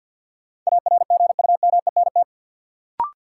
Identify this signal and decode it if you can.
Morse code, heard as a keyed tone, about 47 wpm, 700 Hz. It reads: ARGUMENT
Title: Sentences from Top 1400 Words - New Word Drill Alt - 50wpm